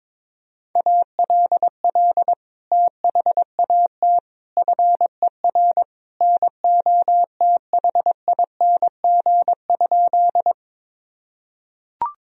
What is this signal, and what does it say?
Morse code, heard as a keyed tone, about 21 wpm, 700 Hz. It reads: ALL THAT FER NOTHING?
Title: Sentences from Top 500 Words and Abbreviations 22wpm